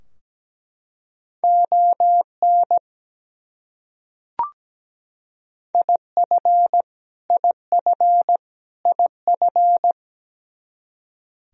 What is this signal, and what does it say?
Morse code, heard as a keyed tone, about 17 wpm, 700 Hz. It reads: ON IF IF IF
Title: Top 100 Words Repeated 17wpm